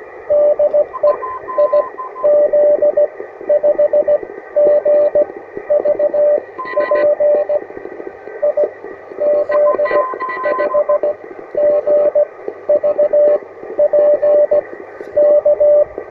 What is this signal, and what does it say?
Morse code, heard as a keyed tone, about 18 wpm, spaced as for 13 wpm, 600 Hz, with radio noise. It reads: DE IZ5GVP IZ5GVP K
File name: attachments / UR4MCK IZ5GVP